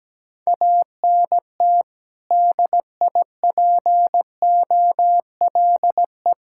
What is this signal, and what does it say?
Morse code, heard as a keyed tone, about 17 wpm, 700 Hz. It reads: ANT DIPOLE
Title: K1USN SST QSO Practice 17wpm